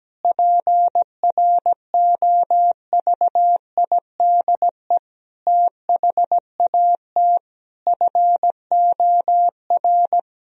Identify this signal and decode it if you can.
Morse code, heard as a keyed tone, about 17 wpm, 700 Hz. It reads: PROVIDE THAT FOR
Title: Sentences from Top 1000 Words - Review Part 3 - 17wpm